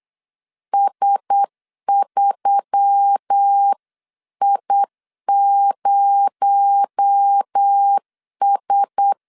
Morse code, heard as a keyed tone, about 8 wpm, 800 Hz. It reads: S3I0S
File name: 8wpm